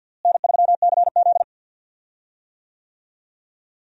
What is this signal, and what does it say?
Morse code, heard as a keyed tone, about 51 wpm, 700 Hz. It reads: N3CZ
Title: USA Call Signs - 4 Characters - Repeated 2x - 50wpm